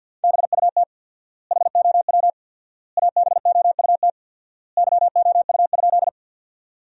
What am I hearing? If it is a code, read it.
Morse code, heard as a keyed tone, about 50 wpm, 700 Hz. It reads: BUT HOW ABOUT YOU?
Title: 4x Word-Spacing - Sentences from Top 100 Words - 50wpm